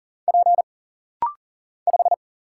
Morse code, heard as a keyed tone, about 40 wpm, 700 Hz. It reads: P 5
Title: Single Letter-Number Alt - Rapid-Fire 40wpm